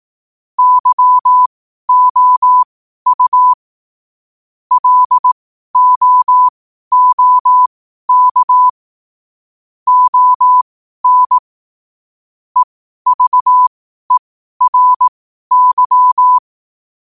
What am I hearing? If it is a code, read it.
Morse code, heard as a keyed tone, about 18 wpm, spaced as for 12 wpm, 1000 Hz. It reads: YOU LOOK ON EVERY
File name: qotdfiles / 12wpm